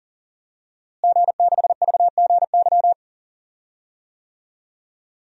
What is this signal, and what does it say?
Morse code, heard as a keyed tone, about 40 wpm, 700 Hz. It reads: G6VGY
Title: DXCC-100 Call Signs - 5 Characters - Repeated 2x - 40wpm